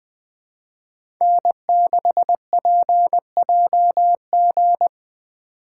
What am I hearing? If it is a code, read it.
Morse code, heard as a keyed tone, about 20 wpm, 700 Hz. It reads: N6PJG
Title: USA Call Signs - 5 Characters - 3x Character Spacing - 20wpm